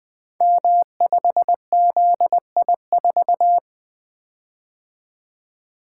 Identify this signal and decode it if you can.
Morse code, heard as a keyed tone, about 20 wpm, 700 Hz. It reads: M5ZI4
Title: Code Groups - 5 Letter:Numbers - 20wpm